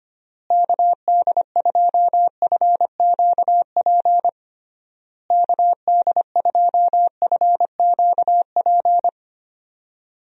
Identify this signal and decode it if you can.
Morse code, heard as a keyed tone, about 25 wpm, 700 Hz. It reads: KD2FQP KD2FQP
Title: USA Call Signs - 6 Characters - Repeated 3x - 25wpm